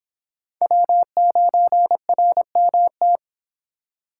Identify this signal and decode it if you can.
Morse code, heard as a keyed tone, about 25 wpm, 700 Hz. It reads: W9RMT